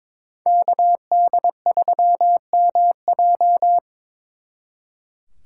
Morse code, heard as a keyed tone, about 25 wpm, 700 Hz. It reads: KD3MJ
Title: Speed Racing - USA 2x2 Call Signs 22wpm